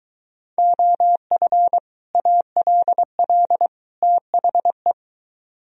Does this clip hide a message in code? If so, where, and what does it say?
Yes: Morse code from 0.6 to 4.9 s, heard as a keyed tone, about 23 wpm, 700 Hz. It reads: OF ALL THE